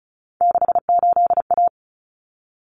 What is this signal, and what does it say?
Morse code, heard as a keyed tone, about 34 wpm, 700 Hz. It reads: 68A
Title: Common Cut Numbers - 3-Digit 35wpm